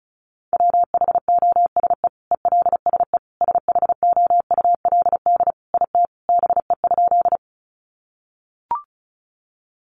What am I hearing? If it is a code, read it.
Morse code, heard as a keyed tone, about 34 wpm, 700 Hz. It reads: WHOSE ELSE SHOULD IT BE?
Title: Sentences from Top 1200 Words 35wpm